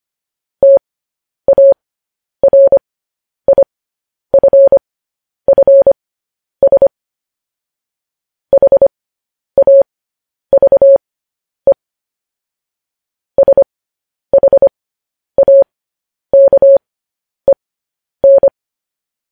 Morse code, heard as a keyed tone, about 25 wpm, spaced as for 10 wpm, 550 Hz. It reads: TARIFFS HAVE SHAKEN
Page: news headlines in morse code at 10 wpm